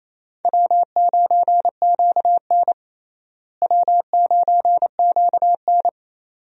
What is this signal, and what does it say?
Morse code, heard as a keyed tone, about 28 wpm, 700 Hz. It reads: W9QN W9QN
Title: USA Call Signs - 4 Characters - Repeated 2x - 28wpm